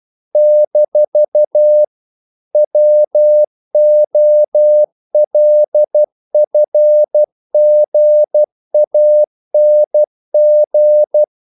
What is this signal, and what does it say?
Morse code, heard as a keyed tone, about 12 wpm, 600 Hz. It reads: - WOLFGANG
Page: English quotations in CW (telegraphy - morse)